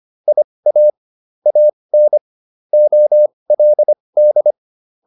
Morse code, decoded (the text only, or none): IA AN OLD